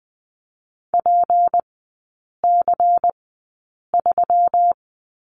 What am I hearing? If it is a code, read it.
Morse code, heard as a keyed tone, about 20 wpm, spaced as for 8 wpm, 700 Hz. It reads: PC3
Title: 152 Lesson 25e - Letter p - Three Characters 20wpm